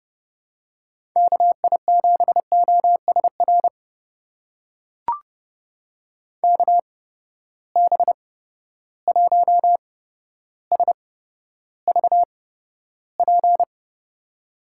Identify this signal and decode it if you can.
Morse code, heard as a keyed tone, about 30 wpm, spaced as for 13 wpm, 700 Hz. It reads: KI7OSR KB1SVP
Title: USA Call Signs - 6 Characters - 8x Character Spacing - 30wpm